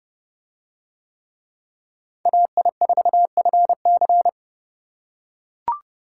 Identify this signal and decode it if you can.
Morse code, heard as a keyed tone, about 32 wpm, 700 Hz. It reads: AI4FC